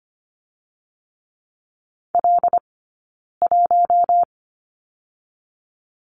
Morse code, heard as a keyed tone, about 25 wpm, 700 Hz. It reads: L 1
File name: Lesson 17d - Letter u - Two Characters 25wpm